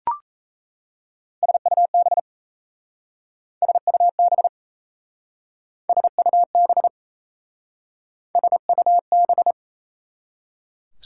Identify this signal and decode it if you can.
Morse code, heard as a keyed tone, about 34 wpm, 690 Hz. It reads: SUB SUB SUB SUB
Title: Speed-Racing Prefixes and Suffixes 28wpm